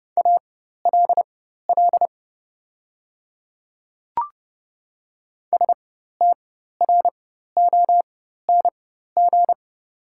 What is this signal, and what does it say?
Morse code, heard as a keyed tone, about 29 wpm, spaced as for 14 wpm, 690 Hz. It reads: ALL STRONG
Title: 4x Character Spacing - Top 500 Words - 30wpm